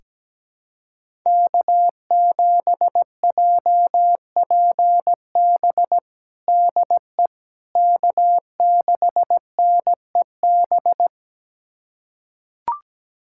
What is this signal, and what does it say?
Morse code, heard as a keyed tone, about 17 wpm, 700 Hz. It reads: K7JPB DE K6NEB
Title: USA Callsign DE Callsign 17wpm